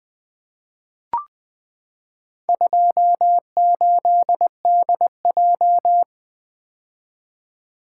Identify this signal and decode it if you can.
Morse code, heard as a keyed tone, about 20 wpm, 700 Hz. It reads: 28DJ
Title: Four Letter-Number Combinations - 20wpm